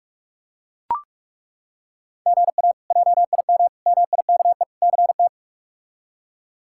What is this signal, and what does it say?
Morse code, heard as a keyed tone, about 41 wpm, 700 Hz. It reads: GA JIM MIKE CT
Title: K1USN SST QSO Practice - 45wpm